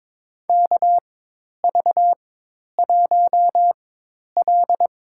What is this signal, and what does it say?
Morse code, heard as a keyed tone, about 22 wpm, spaced as for 10 wpm, 700 Hz. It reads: KV1L
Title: USA Call Signs - 4 Characters - 4x Character Spacing - 22wpm